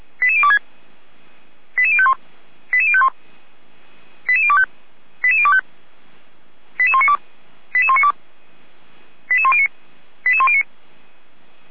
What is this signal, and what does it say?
Five-tone selective call (ZVEI), heard as quick runs of five tones, tones of about 70 ms. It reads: 88026 88051 88051 88025 88025 88192 88192 88109 88109